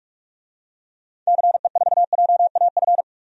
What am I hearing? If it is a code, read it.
Morse code, heard as a keyed tone, about 46 wpm, 700 Hz. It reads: CE4JAF
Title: DXCC-100 Call Signs - 6 Characters - 45wpm